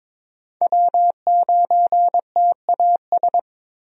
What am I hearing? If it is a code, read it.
Morse code, heard as a keyed tone, about 22 wpm, 700 Hz. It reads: W9TAS
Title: USA Call Signs - 5 Characters - 22wpm